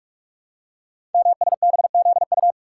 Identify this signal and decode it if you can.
Morse code, heard as a keyed tone, about 44 wpm, 700 Hz. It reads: MSBZU
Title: Code Groups - 5 Letters - 45wpm